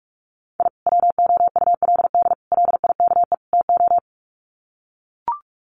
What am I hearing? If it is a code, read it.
Morse code, heard as a keyed tone, about 44 wpm, 700 Hz. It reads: I WOULD LIKE TO
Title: Sets of 4 Words 45wpm